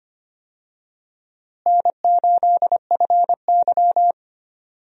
Morse code, heard as a keyed tone, about 25 wpm, 700 Hz. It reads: N8FY
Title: USA Call Signs - 4 Characters - Repeated 3x - 25wpm